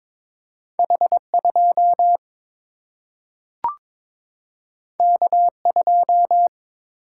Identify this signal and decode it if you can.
Morse code, heard as a keyed tone, about 22 wpm, 700 Hz. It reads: H 2 K 2
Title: Single-Letter Single-Number 22wpm